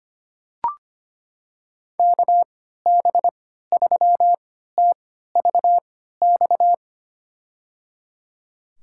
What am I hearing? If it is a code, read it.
Morse code, heard as a keyed tone, about 25 wpm, spaced as for 14 wpm, 700 Hz. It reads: KB3TVX
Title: USA Call Signs - 6 Characters - 3x Character Spacing - 25wpm